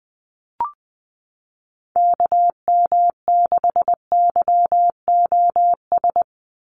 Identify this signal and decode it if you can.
Morse code, heard as a keyed tone, about 20 wpm, 700 Hz. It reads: KM6YOS
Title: USA Call Signs 20wpm